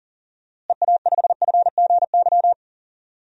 Morse code, heard as a keyed tone, about 40 wpm, 700 Hz. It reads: EA5FGY
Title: DXCC-100 Call Signs - 6 Characters - Repeated 2x - 40wpm